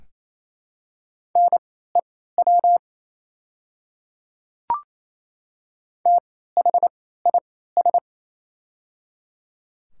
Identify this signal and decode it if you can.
Morse code, heard as a keyed tone, about 28 wpm, spaced as for 16 wpm, 700 Hz. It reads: NEW THIS
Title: 3x Character Spacing - Top 200 Words - 28wpm